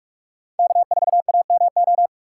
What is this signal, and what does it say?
Morse code, heard as a keyed tone, about 41 wpm, 700 Hz. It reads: K4AMO